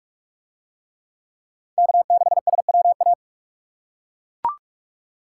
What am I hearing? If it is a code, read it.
Morse code, heard as a keyed tone, about 41 wpm, 700 Hz. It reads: K6SWA